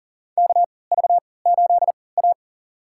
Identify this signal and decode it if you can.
Morse code, heard as a keyed tone, about 40 wpm, spaced as for 23 wpm, 700 Hz. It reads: KV8A
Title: USA Call Signs - 4 Characters - 3x Character Spacing - 40wpm